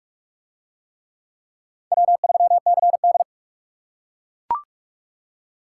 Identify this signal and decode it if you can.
Morse code, heard as a keyed tone, about 44 wpm, 700 Hz. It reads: W3CD